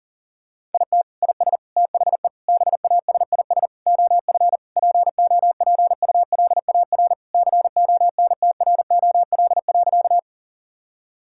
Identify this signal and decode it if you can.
Morse code, heard as a keyed tone, about 40 wpm, 700 Hz. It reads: IT IS THE BASIS OF POPULAR CONTROL.
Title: Sentences from Top 1000 Words - Review Part 5 - 40wpm